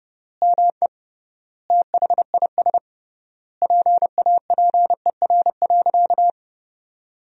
Morse code, heard as a keyed tone, about 30 wpm, 700 Hz. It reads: ME THIS PAPER.
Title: 3x Word-Spacing - Sentences from Top 500 Words - 30wpm